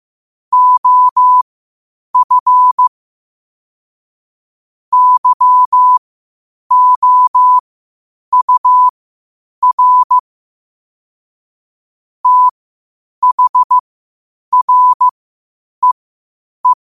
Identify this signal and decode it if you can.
Morse code, heard as a keyed tone, about 15 wpm, spaced as for 8 wpm, 1000 Hz. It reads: OF YOUR THREE
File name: qotdfiles / 8wpm